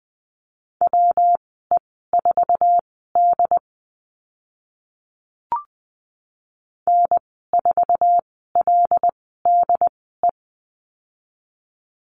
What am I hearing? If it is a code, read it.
Morse code, heard as a keyed tone, about 20 wpm, spaced as for 14 wpm, 700 Hz. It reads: WE4D N4LDE
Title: 147 Lesson 24g - Letter y - Call Signs 20wpm